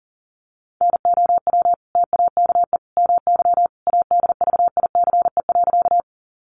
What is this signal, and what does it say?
Morse code, heard as a keyed tone, about 40 wpm, 700 Hz. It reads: NOW TAKE MY ADVICE.
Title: Sentences from Top 1100 Words 40wpm